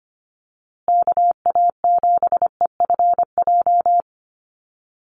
Morse code, heard as a keyed tone, about 25 wpm, 700 Hz. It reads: KA7EFJ
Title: USA Call Signs Repeated 2x 25wpm